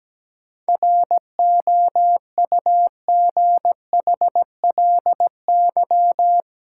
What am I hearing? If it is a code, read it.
Morse code, heard as a keyed tone, about 17 wpm, 700 Hz. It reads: ROUGHLY
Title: Sentences from Top 1500 Words - New Word Drill Alt - 17wpm